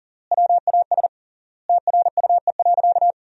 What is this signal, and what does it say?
Morse code, heard as a keyed tone, about 40 wpm, 700 Hz. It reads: WAS TRUE.
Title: 3x Word-Spacing - Sentences from Top 400 Words - 40wpm